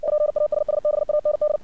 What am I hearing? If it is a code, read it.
Morse code, heard as a keyed tone, about 57 wpm, 600 Hz. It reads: PASSBAND
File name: archived / CRSnet